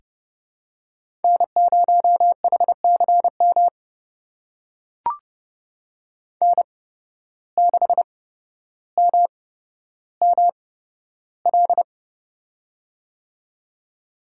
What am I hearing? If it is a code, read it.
Morse code, heard as a keyed tone, about 30 wpm, spaced as for 13 wpm, 700 Hz. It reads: N0HCM N6MML
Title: USA Call Signs - 5 Characters - 8x Character Spacing - 30wpm